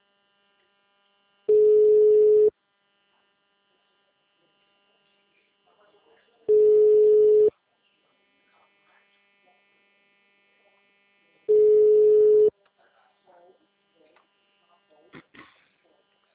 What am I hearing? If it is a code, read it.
Ringback tone (Europe, 425 Hz), with speech.